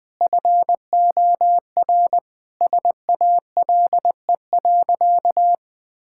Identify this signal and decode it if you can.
Morse code, heard as a keyed tone, about 20 wpm, 700 Hz. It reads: FOR SALE.
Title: Sentences from Top 1600 Words 20wpm